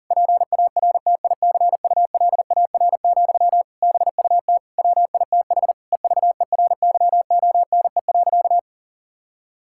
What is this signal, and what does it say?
Morse code, heard as a keyed tone, about 40 wpm, 700 Hz. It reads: PARTICULAR, BUT WITH EVERYONE.
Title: Sentences from Top 1000 Words - Review Part 5 - 40wpm